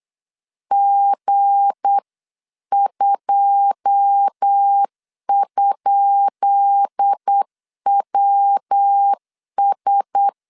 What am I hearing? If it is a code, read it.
Morse code, heard as a keyed tone, about 8 wpm, 800 Hz. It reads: G2?WS
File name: LEZIONI / 10 SIGNS / 10B INT 8wpm 3534